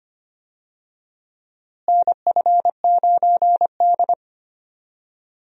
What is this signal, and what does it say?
Morse code, heard as a keyed tone, about 25 wpm, 700 Hz. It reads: NF9D
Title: USA Call Signs - 4 Characters - Repeated 2x - 25wpm